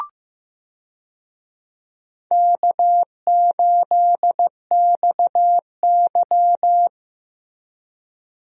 Morse code, heard as a keyed tone, about 15 wpm, 700 Hz. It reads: K8XY
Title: USA Call Signs - 4 Characters - Repeated 3x - 15wpm